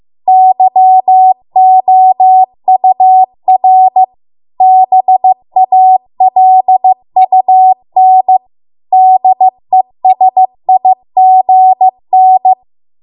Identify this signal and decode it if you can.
Morse code, heard as a keyed tone, about 15 wpm, 750 Hz. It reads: YOUR BALUN DESIGN